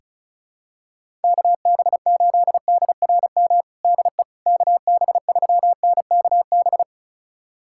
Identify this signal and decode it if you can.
Morse code, heard as a keyed tone, about 34 wpm, 690 Hz. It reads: KB8DRM DE KB3NKB
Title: USA Callsign DE Callsign - 6 Characters - 35wpm